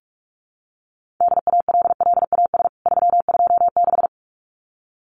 Morse code, heard as a keyed tone, about 44 wpm, 700 Hz. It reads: DALLAS 326
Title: CWT Exchange 45wpm